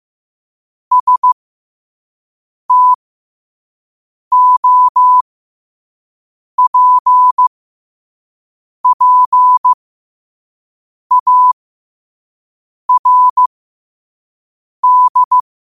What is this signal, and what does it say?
Morse code, heard as a keyed tone, about 15 wpm, spaced as for 5 wpm, 1000 Hz. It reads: STOPPARD
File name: qotdfiles / 5wpm